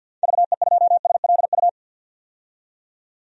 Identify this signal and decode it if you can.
Morse code, heard as a keyed tone, about 52 wpm, 700 Hz. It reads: VE2SLU
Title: DXCC-100 Call Signs - 6 Characters - 50wpm